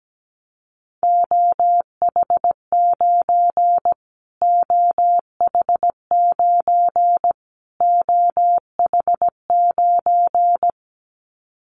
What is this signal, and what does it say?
Morse code, heard as a keyed tone, about 17 wpm, 700 Hz. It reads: OH9 OH9 OH9
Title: Callsign Prefixes for Top 25 DXCC - Repeated 3x - 17wpm